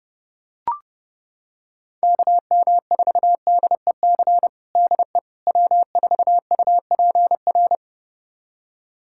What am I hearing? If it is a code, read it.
Morse code, heard as a keyed tone, about 30 wpm, 700 Hz. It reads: KM4DEC DE W4UPR
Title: USA Callsign DE Callsign 30wpm